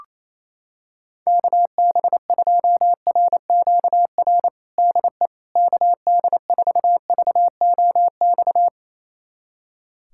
Morse code, heard as a keyed tone, about 28 wpm, 700 Hz. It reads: KB2RQR DE KD4VOX
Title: USA Callsign DE Callsign - 6 Characters - 28wpm